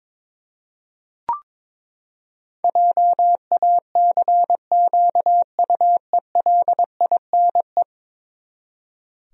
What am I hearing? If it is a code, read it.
Morse code, heard as a keyed tone, about 22 wpm, 700 Hz. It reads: JACQUELINE
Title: USA Common Names - 22wpm